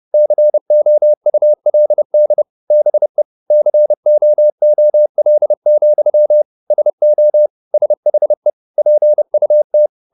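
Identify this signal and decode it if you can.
Morse code, heard as a keyed tone, about 30 wpm, 600 Hz. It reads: COULD BE COOL, SO SHE PUT
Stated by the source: Morse code at 30 wpm, the novel Never by Ken Follett in English, from CW med Gnister.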